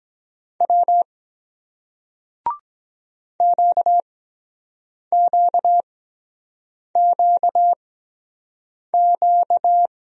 Morse code, heard as a keyed tone, about 23 wpm, spaced as for 7 wpm, 700 Hz. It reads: W QQQQ